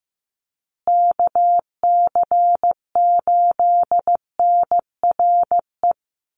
Morse code, heard as a keyed tone, about 15 wpm, 700 Hz. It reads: KC8NRE
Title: USA Call Signs Repeated 3x 15wpm